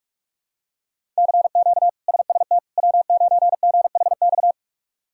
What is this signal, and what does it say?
Morse code, heard as a keyed tone, about 41 wpm, 700 Hz. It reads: CQ SST W9GHX
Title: K1USN SST QSO Practice - 45wpm